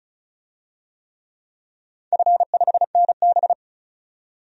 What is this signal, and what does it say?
Morse code, heard as a keyed tone, about 36 wpm, 700 Hz. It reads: F5NB
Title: DXCC-100 Call Signs - 4 Characters - 35wpm